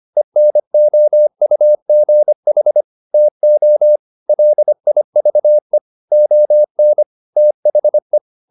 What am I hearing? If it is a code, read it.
Morse code, heard as a keyed tone, about 25 wpm, 600 Hz. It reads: ENOUGH TO LIVE ON THE